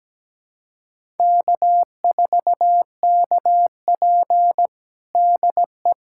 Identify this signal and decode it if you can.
Morse code, heard as a keyed tone, about 17 wpm, 700 Hz. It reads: K4KP DE
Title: USA Callsign DE Callsign 17wpm